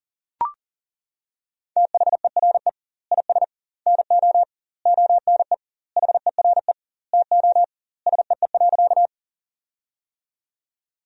Morse code, heard as a keyed tone, about 40 wpm, 700 Hz. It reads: THERE IS NO ONE HERE TO SEE.
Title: 2x Word-Spacing - Sentences from Top 200 Words - 40wpm